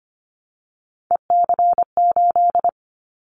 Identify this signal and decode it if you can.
Morse code, heard as a keyed tone, about 25 wpm, 700 Hz. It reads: EC8
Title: Callsign Prefixes for Top 50 DXCC - 25wpm